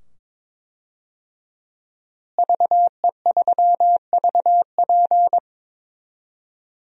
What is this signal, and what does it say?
Morse code, heard as a keyed tone, about 22 wpm, 700 Hz. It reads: VE3VP
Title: DXCC-100 Call Signs - 5 Characters - Repeated 2x - 22wpm